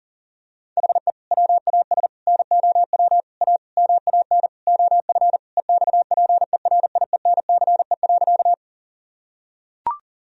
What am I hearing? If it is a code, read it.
Morse code, heard as a keyed tone, about 40 wpm, 700 Hz. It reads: HE WAS NOW A MAN OF EXPERIENCE.
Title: Sentences from Top 1000 Words - Review Part 3 - 40wpm